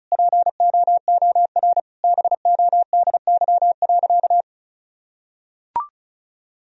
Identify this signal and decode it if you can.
Morse code, heard as a keyed tone, about 36 wpm, 700 Hz. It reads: POOR BODY.
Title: Sentences from Top 1000 Words - Review Part 2 - 35wpm